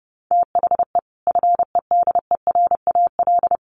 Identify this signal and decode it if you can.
Morse code, heard as a keyed tone, about 30 wpm, 700 Hz. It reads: THE FEDERAL
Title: Sets of 3 Words 30wpm